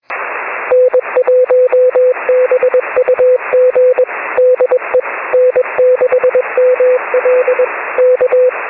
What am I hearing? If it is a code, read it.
Morse code, heard as a keyed tone, about 21 wpm, 500 Hz. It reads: N1BUG DE N6ML K
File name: 6m-n6ml